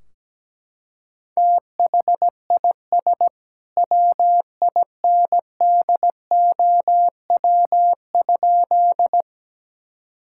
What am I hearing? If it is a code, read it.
Morse code, heard as a keyed tone, about 17 wpm, 700 Hz. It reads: THIS WINDOW?
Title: Sentences from Top 1600 Words 17wpm